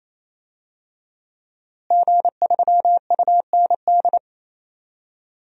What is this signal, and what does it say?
Morse code, heard as a keyed tone, about 28 wpm, 700 Hz. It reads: G3UND
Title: DXCC-100 Call Signs - 5 Characters - Repeated 3x - 28wpm